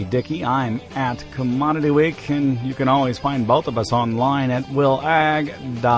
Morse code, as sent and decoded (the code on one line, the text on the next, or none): none